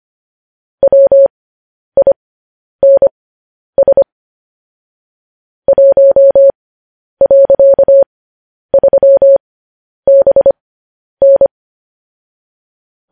Morse code, heard as a keyed tone, about 25 wpm, spaced as for 10 wpm, 550 Hz. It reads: WINS 1.3BN